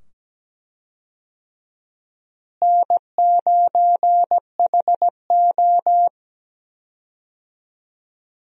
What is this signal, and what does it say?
Morse code, heard as a keyed tone, about 17 wpm, 700 Hz. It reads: N9HO
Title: USA Call Signs - 4 Characters - Repeated 2x - 17wpm